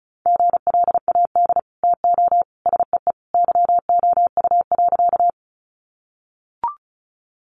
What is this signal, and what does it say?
Morse code, heard as a keyed tone, about 34 wpm, 700 Hz. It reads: GLAD TO SEE YOU.
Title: Sentences from Top 800 Words 35wpm